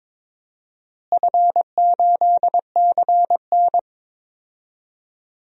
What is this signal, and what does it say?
Morse code, heard as a keyed tone, about 23 wpm, 700 Hz. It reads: F8CN
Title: DXCC-100 Call Signs - 4 Characters - Repeated 3x - 22wpm